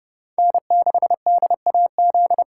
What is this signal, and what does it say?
Morse code, heard as a keyed tone, about 35 wpm, 700 Hz. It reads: N6DAZ
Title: Speed Racing - USA 1x3 Call Signs 30wpm